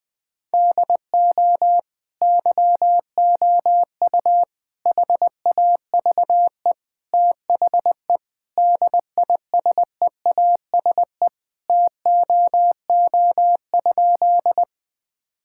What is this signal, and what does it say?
Morse code, heard as a keyed tone, about 21 wpm, 700 Hz. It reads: DO YOU HAVE THE DISEASE TOO?
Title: Sentences from Top 1000 Words - Review Part 6 - 20wpm